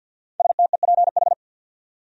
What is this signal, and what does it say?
Morse code, heard as a keyed tone, about 50 wpm, 700 Hz. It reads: STEPH
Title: USA Common Names - 50wpm